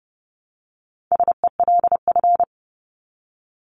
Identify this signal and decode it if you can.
Morse code, heard as a keyed tone, about 30 wpm, 700 Hz. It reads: SELF
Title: Top 500 Words 30wpm